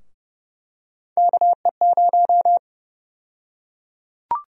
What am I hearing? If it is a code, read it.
Morse code, heard as a keyed tone, about 30 wpm, 700 Hz. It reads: KE0
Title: USA Call Sign Prefixes - Repeated 2x - 30wpm